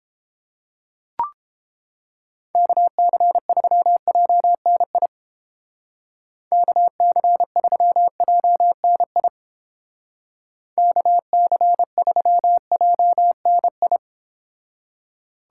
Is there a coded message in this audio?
Yes: Morse code from 2.5 to 14.0 s, heard as a keyed tone, about 31 wpm, 700 Hz. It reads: KC3JNI KC3JNI KC3JNI